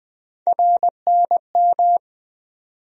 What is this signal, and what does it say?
Morse code, heard as a keyed tone, about 20 wpm, 700 Hz. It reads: RNM
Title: USA Call Sign Postfixes - 3 Letters 20wpm